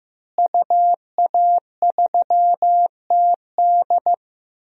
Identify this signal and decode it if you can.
Morse code, heard as a keyed tone, about 15 wpm, 700 Hz. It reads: UA3TD
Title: DXCC-100 Call Signs - 5 Characters - Repeated 3x - 15wpm